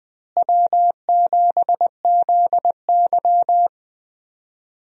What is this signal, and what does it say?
Morse code, heard as a keyed tone, about 20 wpm, 700 Hz. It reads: W7ZY
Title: USA Call Signs - 4 Characters - Repeated 3x - 20wpm